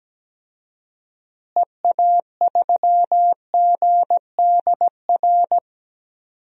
Morse code, heard as a keyed tone, about 17 wpm, 700 Hz. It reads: EA3GDR
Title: DXCC-100 Call Signs - 6 Characters - Repeated 2x - 17wpm